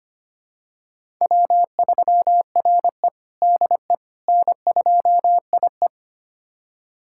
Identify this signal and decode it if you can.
Morse code, heard as a keyed tone, about 25 wpm, 700 Hz. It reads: W3RE DE N2IE
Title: USA Callsign DE Callsign - 4 Characters - 25wpm